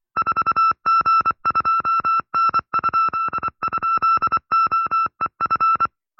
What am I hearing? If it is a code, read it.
Morse code, heard as a keyed tone, about 25 wpm, 1340 Hz. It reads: 4G2N??OEF